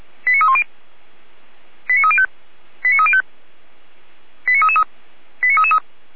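Five-tone selective call (ZVEI), heard as quick runs of five tones, tones of about 70 ms. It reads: 89310 89305 89305 89333 89333
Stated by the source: ZVEI five-tone call received on SDR.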